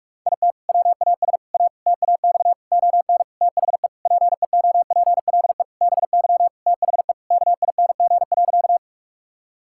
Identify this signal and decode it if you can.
Morse code, heard as a keyed tone, about 46 wpm, 700 Hz. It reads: IT WAS A TAX ON THE PEOPLE BY THE KING.
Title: Sentences from Top 1000 Words - Review Part 5 - 45wpm